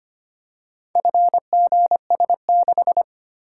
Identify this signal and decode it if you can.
Morse code, heard as a keyed tone, about 25 wpm, 700 Hz. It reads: FGS6
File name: practice / 25wpm / Four Letter-Number Combinations - 25wpm